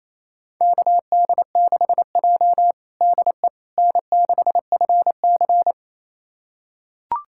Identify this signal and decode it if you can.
Morse code, heard as a keyed tone, about 28 wpm, 690 Hz. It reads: KD6J DE N6FC
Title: USA Callsign DE Callsign - 4 Characters - 28wpm